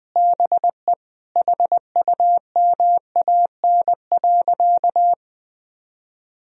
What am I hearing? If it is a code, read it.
Morse code, heard as a keyed tone, about 20 wpm, 700 Hz. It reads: BE HUMAN.
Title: Sentences from Top 1000 Words - Review Part 3 - 20wpm